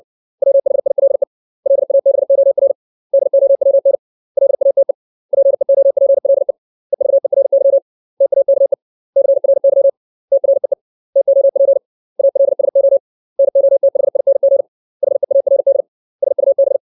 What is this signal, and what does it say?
Morse code, heard as a keyed tone, about 64 wpm, 550 Hz. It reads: WHILE LABOR DOWN LATE PEOPLE EVERY TAKE CRY TREE TOP ALSO NOTHING HEARD SUB
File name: 632 words / 632 Words 60WPM